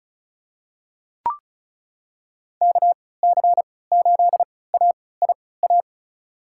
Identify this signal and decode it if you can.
Morse code, heard as a keyed tone, about 36 wpm, spaced as for 20 wpm, 700 Hz. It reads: KC8AIA
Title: USA Call Signs - 3x Character Spacing - 35wpm